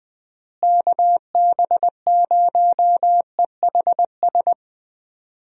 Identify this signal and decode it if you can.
Morse code, heard as a keyed tone, about 20 wpm, 700 Hz. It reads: KB0EHS
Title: USA Call Signs - 6 Characters - Repeated 2x - 20wpm